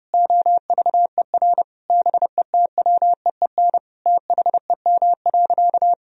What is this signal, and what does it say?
Morse code, heard as a keyed tone, about 29 wpm, 700 Hz. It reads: OVER BETWEEN THEM.
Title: Sentences from Top 1000 Words - Review Part 1 - 30wpm